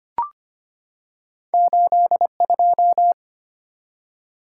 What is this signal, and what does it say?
Morse code, heard as a keyed tone, about 25 wpm, 700 Hz. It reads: 82
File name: Two Letter-Number Combinations - 25wpm